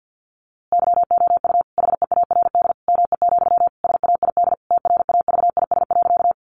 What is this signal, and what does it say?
Morse code, heard as a keyed tone, about 50 wpm, 700 Hz. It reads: YOU HEARD ME, SAID TRAVIS.
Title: A Sound Of Thunder 50wpm